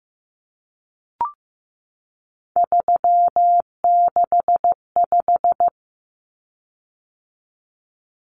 Morse code, heard as a keyed tone, about 15 wpm, 700 Hz. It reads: 365
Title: Common Cut Numbers - 3-Digit 15wpm